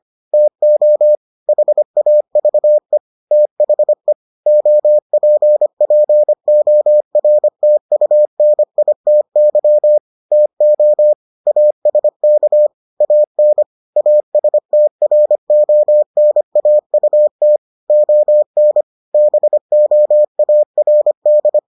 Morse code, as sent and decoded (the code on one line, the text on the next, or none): - --- / .... .- ...- . / - .... / --- .--. .--. --- .-. - ..- -. .. - -.-- / - --- / .- ... -.- / .- -. / .- ... - .-. --- -. .- ..- - / --- -. / -... --- .- .-. -..
TO HAVE THE OPPORTUNITY TO ASK AN ASTRONAUT ON BOARD